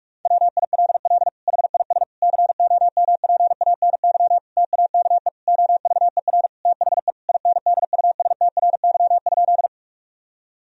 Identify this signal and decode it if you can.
Morse code, heard as a keyed tone, about 45 wpm, 700 Hz. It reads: WILL HIS COMPANY TAKE OVER THE INDUSTRY?
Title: Sentences from Top 1000 Words - Review Part 4 - 45wpm